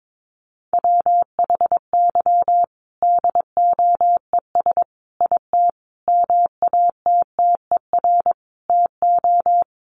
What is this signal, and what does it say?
Morse code, heard as a keyed tone, about 22 wpm, 700 Hz. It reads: WHY DOES IT MATTER TO
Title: Sentences from Top 700 Words 22wpm